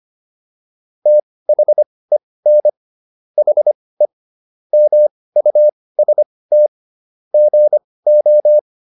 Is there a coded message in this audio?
Yes: Morse code from 1.1 to 8.6 s, heard as a keyed tone, about 25 wpm, spaced as for 18 wpm, 600 Hz. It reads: THEN HE MUST GO